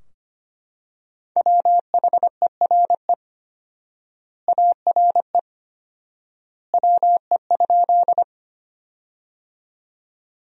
Morse code, heard as a keyed tone, about 25 wpm, 700 Hz. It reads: WHERE ARE WE?
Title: 4x Word-Spacing - Sentences from Top 200 Words - 25wpm